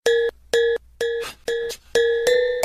There is no code.